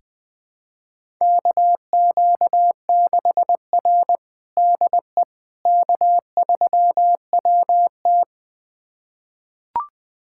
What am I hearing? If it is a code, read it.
Morse code, heard as a keyed tone, about 20 wpm, 700 Hz. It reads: KQ6R DE K3WT
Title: USA Callsign DE Callsign - 4 Characters - 20wpm